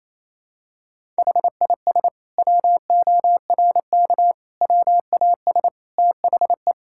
Morse code, heard as a keyed tone, about 28 wpm, 700 Hz. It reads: HIS WORK WAS THE
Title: Sentences from Top 1000 Words - Review Part 5 - 28wpm